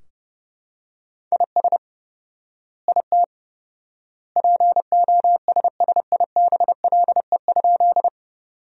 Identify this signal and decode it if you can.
Morse code, heard as a keyed tone, about 32 wpm, 700 Hz. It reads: IS IT POSSIBLE?
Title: 4x Word-Spacing - Sentences from Top 500 Words - 30wpm